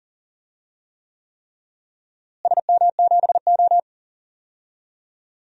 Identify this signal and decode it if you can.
Morse code, heard as a keyed tone, about 40 wpm, 700 Hz. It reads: SM7O
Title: DXCC-100 Call Signs - 4 Characters - Repeated 3x - 40wpm